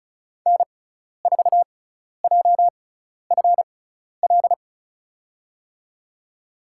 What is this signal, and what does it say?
Morse code, heard as a keyed tone, about 34 wpm, spaced as for 12 wpm, 700 Hz. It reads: N4JFL
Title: USA Call Signs - 6x Character Spacing - 35wpm